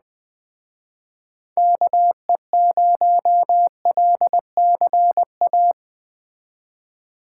Morse code, heard as a keyed tone, about 20 wpm, 700 Hz. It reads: KE0LCA